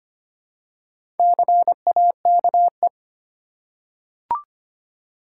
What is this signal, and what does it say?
Morse code, heard as a keyed tone, about 25 wpm, 690 Hz. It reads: CAKE
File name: practice / 25wpm / Easy Words with Hints - 25wpm